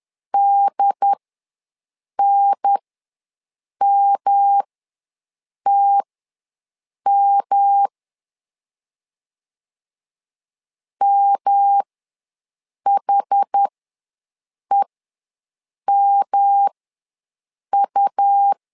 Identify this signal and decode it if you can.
Morse code, heard as a keyed tone, about 10 wpm, spaced as for 6 wpm, 800 Hz. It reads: DNMTM MHEMU
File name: LEZIONI / 3 MWRO / 3 M ws 10wpm 2121